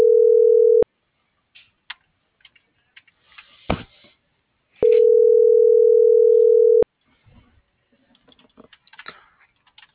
Ringback tone (North America), 440 + 480 Hz.